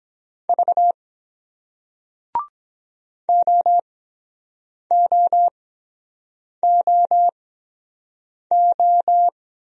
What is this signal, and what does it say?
Morse code, heard as a keyed tone, about 23 wpm, spaced as for 7 wpm, 700 Hz. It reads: V OOOO